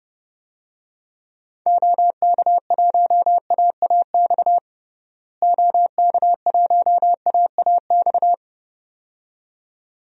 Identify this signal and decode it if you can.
Morse code, heard as a keyed tone, about 30 wpm, 700 Hz. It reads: OK1AAX OK1AAX
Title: DXCC-100 Call Signs - 6 Characters - Repeated 2x - 30wpm